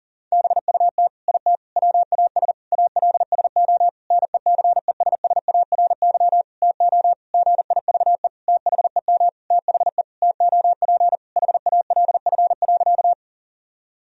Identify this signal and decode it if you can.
Morse code, heard as a keyed tone, about 40 wpm, 700 Hz. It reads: BUT IT WAS ALSO NECESSARY TO GIVE THEM THE TOP HALF.
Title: Sentences from Top 1000 Words - Review Part 2 - 40wpm